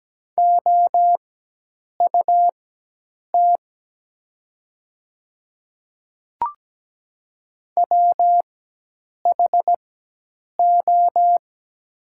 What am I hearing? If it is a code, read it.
Morse code, heard as a keyed tone, about 17 wpm, spaced as for 8 wpm, 690 Hz. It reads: OUT WHO
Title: 4x Character Spacing - Top 200 Words - 17wpm